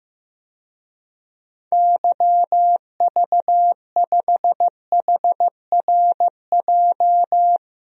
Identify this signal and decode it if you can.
Morse code, heard as a keyed tone, about 15 wpm, 700 Hz. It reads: YV5HRJ
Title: DXCC-100 Call Signs - 6 Characters - 15wpm